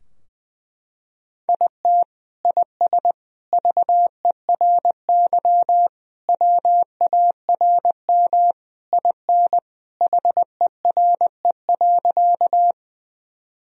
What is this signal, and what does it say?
Morse code, heard as a keyed tone, about 20 wpm, 700 Hz. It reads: IT IS VERY WARM IN HERE.
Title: Sentences from Top 1000 Words - Review Part 2 - 20wpm